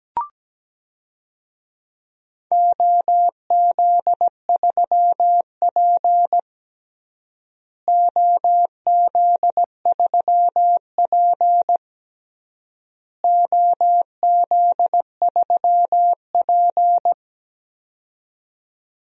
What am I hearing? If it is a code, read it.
Morse code, heard as a keyed tone, about 17 wpm, 700 Hz. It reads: OZ3P OZ3P OZ3P
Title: DXCC-100 Call Signs - 4 Characters - Repeated 3x - 17wpm